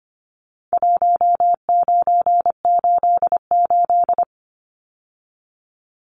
Morse code, heard as a keyed tone, about 24 wpm, 700 Hz. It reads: 1988
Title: Numbers - Contest Style - 4-Digit 25wpm